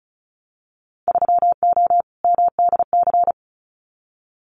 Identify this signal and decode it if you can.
Morse code, heard as a keyed tone, about 34 wpm, 700 Hz. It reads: 3O MDC
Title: Winter Field Day Exchanges Encore 35wpm